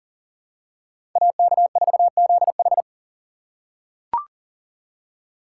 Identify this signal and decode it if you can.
Morse code, heard as a keyed tone, about 40 wpm, 700 Hz. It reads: AK4ZH
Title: USA Call Signs - 5 Characters - 4x Character Spacing - 40wpm